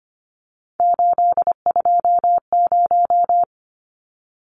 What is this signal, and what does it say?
Morse code, heard as a keyed tone, about 25 wpm, 700 Hz. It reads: 820
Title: Numbers - 3-Digit 25wpm